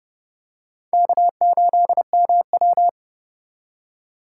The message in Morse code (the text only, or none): K8MW